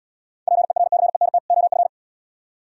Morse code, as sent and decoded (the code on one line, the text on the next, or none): .--. .-. -.-. -. - / --- ..-.
PERCENT OF